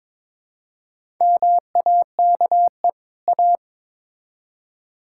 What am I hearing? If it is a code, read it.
Morse code, heard as a keyed tone, about 22 wpm, 700 Hz. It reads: MAKE A
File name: Build A Phrase - 2-Word Phrases - 22wpm